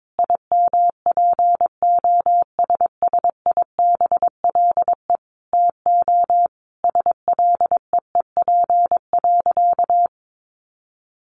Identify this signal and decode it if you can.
Morse code, heard as a keyed tone, about 22 wpm, 700 Hz. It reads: IMPOSSIBLE TO SLEEP.
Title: Sentences from Top 1100 Words 22wpm